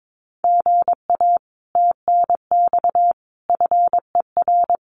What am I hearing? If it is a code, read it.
Morse code, heard as a keyed tone, about 22 wpm, 700 Hz. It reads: GA TNX FER
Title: Common QSO Phrases Encore 22wpm